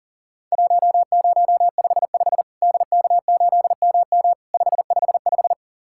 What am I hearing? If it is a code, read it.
Morse code, heard as a keyed tone, about 40 wpm, 700 Hz. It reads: 1055 DK8MM 555
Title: Worked All Europe DX Contest - QTC Exchange - 40wpm